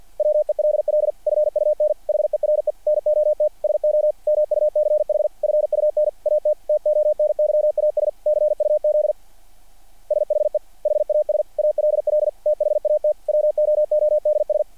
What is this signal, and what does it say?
Morse code, heard as a keyed tone, about 50 wpm, 600 Hz. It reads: WELL FUN HERE NOT SO MUCH FUN AT TONYAS CUZ SHE HAS ALL THAT WOODS
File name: archived / CRSnet